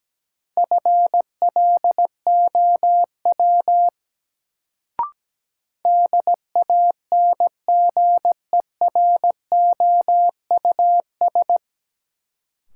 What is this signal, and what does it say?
Morse code, heard as a keyed tone, about 17 wpm, 700 Hz. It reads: FLOW DANGEROUS
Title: Sentences from Top 1200 Words - New Word Drill Alt - 17wpm